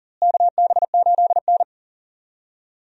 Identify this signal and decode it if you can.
Morse code, heard as a keyed tone, about 40 wpm, 700 Hz. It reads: KB8N
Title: USA Call Signs - 4 Characters - Repeated 3x - 40wpm